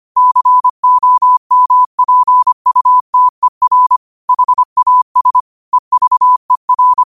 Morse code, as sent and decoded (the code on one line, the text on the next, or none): -.-. --- -- .--. ..- - . .-. / .... .- ... / . ...- . .-.
COMPUTER HAS EVER